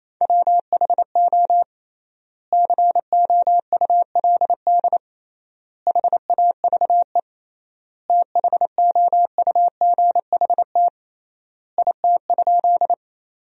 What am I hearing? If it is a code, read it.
Morse code, heard as a keyed tone, about 28 wpm, 700 Hz. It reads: WHO COULD HAVE THOUGHT IT?
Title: 3x Word-Spacing - Sentences from Top 300 Words - 28wpm